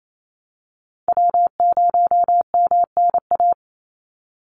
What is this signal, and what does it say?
Morse code, heard as a keyed tone, about 28 wpm, 700 Hz. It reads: W0MNA
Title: NA SOTA Chaser Callsigns 28wpm